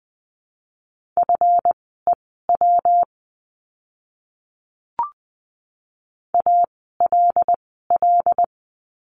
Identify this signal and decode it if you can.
Morse code, heard as a keyed tone, about 20 wpm, spaced as for 14 wpm, 700 Hz. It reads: FEW ALL
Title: 139 Lesson 23f - Letter f - Words 20wpm